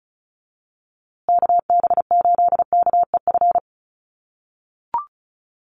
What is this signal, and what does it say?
Morse code, heard as a keyed tone, about 34 wpm, 700 Hz. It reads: KB8KEF
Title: USA Call Signs Repeated 3x 35wpm